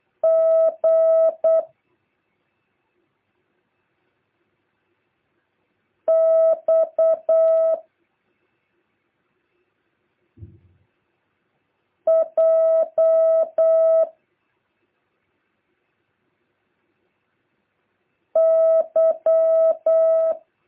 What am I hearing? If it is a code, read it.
Morse code, heard as a keyed tone, about 8 wpm, 650 Hz. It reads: G X J Y
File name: jh8cbh / 2023cw